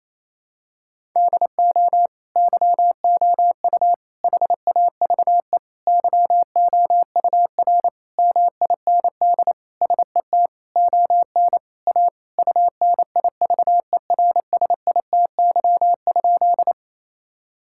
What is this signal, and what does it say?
Morse code, heard as a keyed tone, about 28 wpm, 700 Hz. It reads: DO YOU HAVE YOUR MIND SET ON A UNIVERSITY?
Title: Sentences from Top 1000 Words - Review Part 5 - 28wpm